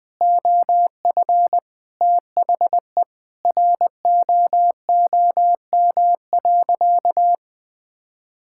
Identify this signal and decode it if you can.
Morse code, heard as a keyed tone, about 21 wpm, 700 Hz. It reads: OF THE ROOM.
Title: Sentences from Top 1000 Words - Review Part 6 - 20wpm